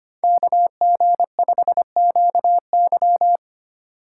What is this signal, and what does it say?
Morse code, heard as a keyed tone, about 25 wpm, 700 Hz. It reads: KG5QY
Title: USA Call Signs - 5 Characters - 2x Character Spacing - 25wpm